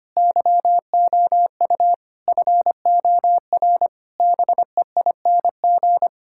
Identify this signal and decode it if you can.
Morse code, heard as a keyed tone, about 25 wpm, 700 Hz. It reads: YOU FOR BEING
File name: Sets of 9 Words - 25wpm